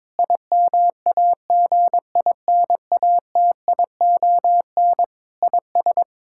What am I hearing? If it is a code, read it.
Morse code, heard as a keyed tone, about 21 wpm, 700 Hz. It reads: IMAGINATION IS
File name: Difficult Sentences with Hints - 22wpm